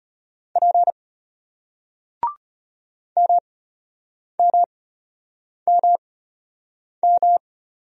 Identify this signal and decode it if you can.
Morse code, heard as a keyed tone, about 33 wpm, spaced as for 8 wpm, 690 Hz. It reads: P MMMM